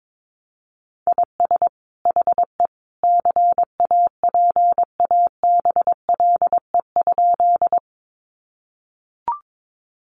Morse code, heard as a keyed tone, about 22 wpm, 700 Hz. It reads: IS HE CAPABLE?
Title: Sentences from Top 1200 Words 22wpm